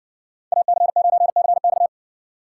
Morse code, heard as a keyed tone, about 60 wpm, 700 Hz. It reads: AC9CX